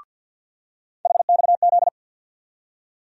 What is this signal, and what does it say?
Morse code, heard as a keyed tone, about 52 wpm, 700 Hz. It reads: HXZ
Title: USA Call Sign Postfixes - 3 Letters 50wpm